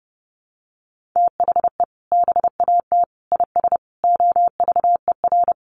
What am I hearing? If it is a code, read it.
Morse code, heard as a keyed tone, about 30 wpm, 700 Hz. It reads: THE BAT IS OVER
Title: Sentences from Top 1500 Words 30wpm